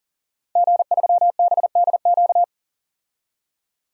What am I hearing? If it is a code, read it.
Morse code, heard as a keyed tone, about 40 wpm, 700 Hz. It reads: G3BDQ
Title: DXCC-100 Call Signs - 5 Characters - 40wpm